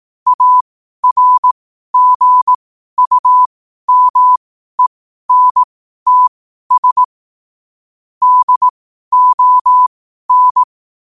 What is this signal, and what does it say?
Morse code, heard as a keyed tone, about 18 wpm, spaced as for 12 wpm, 1000 Hz. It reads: ARGUMENTS DON
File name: qotdfiles / 12wpm